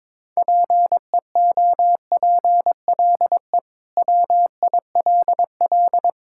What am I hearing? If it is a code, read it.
Morse code, heard as a keyed tone, about 22 wpm, 700 Hz. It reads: PEOPLE WILL
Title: Sentences from Top 1000 Words - Review Part 6 - 22wpm